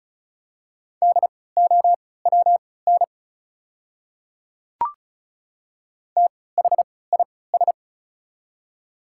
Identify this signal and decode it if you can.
Morse code, heard as a keyed tone, about 36 wpm, spaced as for 20 wpm, 700 Hz. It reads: DOWN THIS
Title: 3x Character Spacing - Top 200 Words - 35wpm